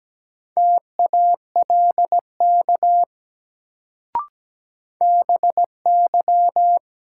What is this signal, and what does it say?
Morse code, heard as a keyed tone, about 17 wpm, 700 Hz. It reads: TALK BY
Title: Sets of 2 Words - New Word Drill - 17wpm